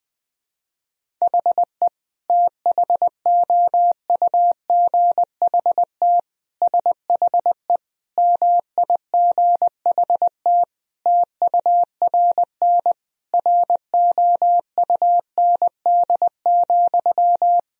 Morse code, heard as a keyed tone, about 21 wpm, 700 Hz. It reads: HE THOUGHT SHE MIGHT TURN ROUND,
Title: Sentences from Top 1000 Words - Review Part 1 - 20wpm